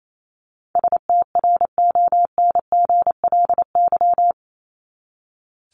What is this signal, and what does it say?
Morse code, heard as a keyed tone, about 28 wpm, 700 Hz. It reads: STRONGLY
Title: LY Suffix - Top 100 Words with Base Words 28wpm